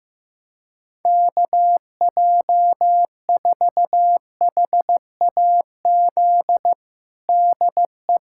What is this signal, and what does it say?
Morse code, heard as a keyed tone, about 16 wpm, 700 Hz. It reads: KJ4HAZ DE
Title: USA Callsign DE Callsign - 6 Characters - 15wpm